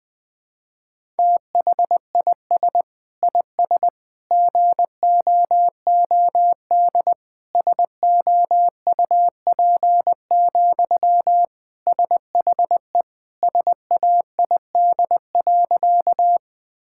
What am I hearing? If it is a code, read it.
Morse code, heard as a keyed tone, about 20 wpm, 700 Hz. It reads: THIS IS GOOD SOUP, SHE SAID.
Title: Sentences from Top 1000 Words - Review Part 6 - 20wpm